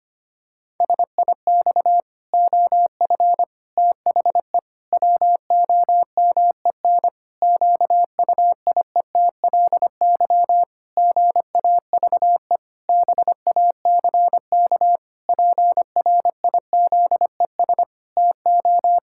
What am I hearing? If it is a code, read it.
Morse code, heard as a keyed tone, about 25 wpm, 700 Hz. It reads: SIX OF THE WOMEN QUIETLY GAVE BACK PRIZES TO
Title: Pangram Challenge v2 - 25wpm